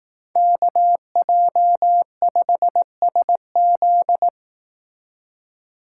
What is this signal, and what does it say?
Morse code, heard as a keyed tone, about 18 wpm, 700 Hz. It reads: KJ5SZ
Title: Speed Racing - USA Call Signs 15wpm